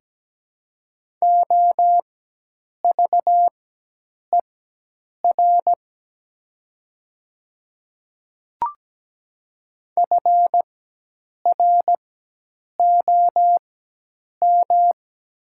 Morse code, heard as a keyed tone, about 17 wpm, spaced as for 8 wpm, 700 Hz. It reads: OVER FROM
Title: 4x Character Spacing - Top 100 Words - 17wpm